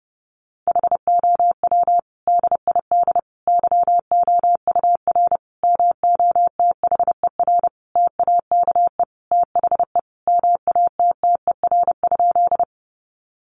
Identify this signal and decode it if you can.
Morse code, heard as a keyed tone, about 30 wpm, 700 Hz. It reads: HOW DID YOUR MOTHER TAKE THE MATTER?
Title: Sentences from Top 700 Words 30wpm